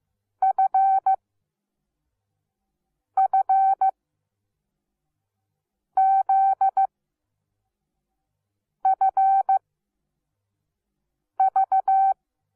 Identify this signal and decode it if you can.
Morse code, heard as a keyed tone, about 15 wpm, spaced as for 4 wpm, 770 Hz. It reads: FFZFV
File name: Onlinelehrgang / morsen / DJ4UF Morsen17